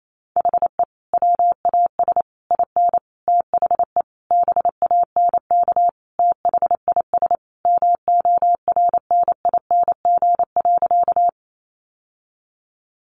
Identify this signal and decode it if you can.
Morse code, heard as a keyed tone, about 28 wpm, 700 Hz. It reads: HE WAS IN THE BANK THIS MORNING.
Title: Sentences from Top 900 Words 28wpm